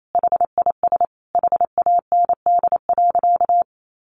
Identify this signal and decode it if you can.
Morse code, heard as a keyed tone, about 28 wpm, 700 Hz. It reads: HIS HAND.
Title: Sentences from Top 1400 Words 28wpm